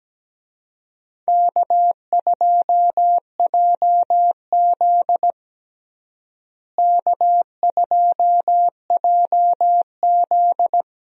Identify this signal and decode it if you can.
Morse code, heard as a keyed tone, about 17 wpm, 700 Hz. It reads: K2JZ K2JZ
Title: USA Call Signs - 4 Characters - Repeated 3x - 17wpm